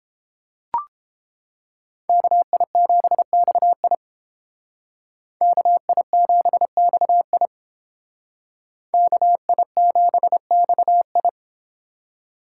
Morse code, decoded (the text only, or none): KI7XI KI7XI KI7XI